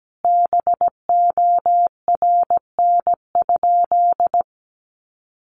Morse code, heard as a keyed tone, about 17 wpm, 700 Hz. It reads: BORN?
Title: Sentences from Top 1000 Words 17wpm